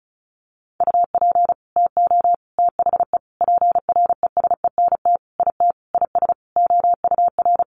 Morse code, heard as a keyed tone, about 34 wpm, 700 Hz. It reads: UP TO THE PRESENT IT IS OUR
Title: Sentences from Top 800 Words 35wpm